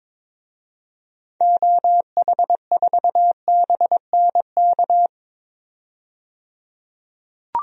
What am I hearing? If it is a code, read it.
Morse code, heard as a keyed tone, about 22 wpm, 700 Hz. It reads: OH4BNK
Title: DXCC-100 Call Signs - 6 Characters - 22wpm